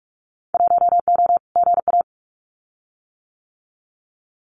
Morse code, heard as a keyed tone, about 44 wpm, 700 Hz. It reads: 1O GA
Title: Winter Field Day Exchanges 45wpm